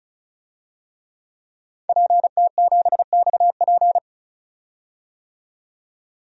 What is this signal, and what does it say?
Morse code, heard as a keyed tone, about 36 wpm, 700 Hz. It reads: PT7XP
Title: DXCC-100 Call Signs - 5 Characters - 35wpm